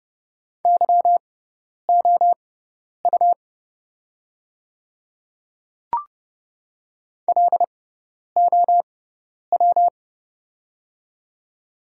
Morse code, heard as a keyed tone, about 30 wpm, spaced as for 10 wpm, 700 Hz. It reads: YOU LOW
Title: 6x Character Spacing - Top 300 Words - 30wpm